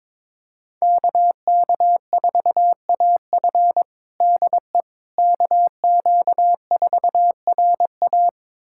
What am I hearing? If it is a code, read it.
Morse code, heard as a keyed tone, about 23 wpm, 700 Hz. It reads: KK4AF DE KQ4RA
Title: USA Callsign DE Callsign - 5 Characters - 22wpm